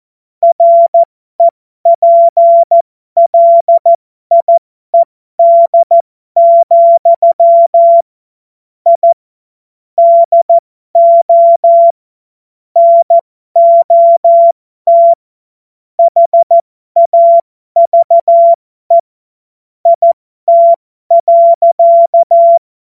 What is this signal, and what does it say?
Morse code, heard as a keyed tone, about 14 wpm, spaced as for 12 wpm, 680 Hz. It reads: REPLIED, I DO NOT HAVE IT.
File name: Morsekurs / Witze